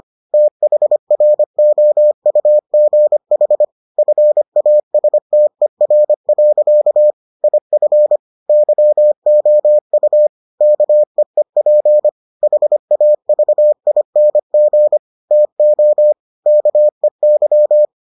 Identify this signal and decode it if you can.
Morse code, heard as a keyed tone, about 25 wpm, 600 Hz. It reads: THROUGH FASTER. IF YOU KEEP HAVING TO KEY